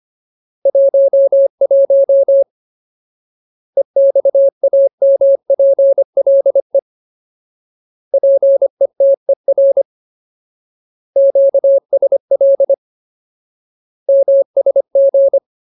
Morse code, heard as a keyed tone, about 25 wpm, 550 Hz. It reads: 11 EXAMPLE PETER QSL MSG
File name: Ultimate Word List / 25WPM Ultimate